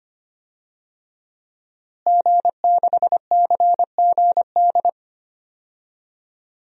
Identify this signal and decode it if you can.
Morse code, heard as a keyed tone, about 25 wpm, 700 Hz. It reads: G6CGD
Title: DXCC-100 Call Signs - 5 Characters - Repeated 2x - 25wpm